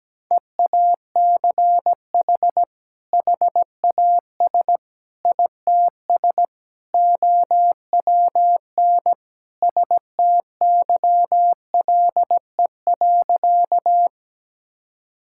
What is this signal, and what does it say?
Morse code, heard as a keyed tone, about 17 wpm, 700 Hz. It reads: EACH HAS ITS OWN STYLE.
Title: Sentences from Top 1000 Words - Review Part 5 - 17wpm